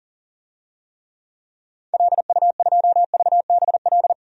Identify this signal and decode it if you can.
Morse code, heard as a keyed tone, about 40 wpm, 700 Hz. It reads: LU2VBL
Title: DXCC-100 Call Signs - 6 Characters - 40wpm